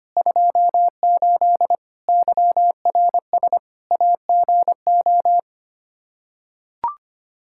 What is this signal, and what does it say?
Morse code, heard as a keyed tone, about 25 wpm, 700 Hz. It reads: 28 YRS AGO
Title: K1USN SST QSO Practice 25wpm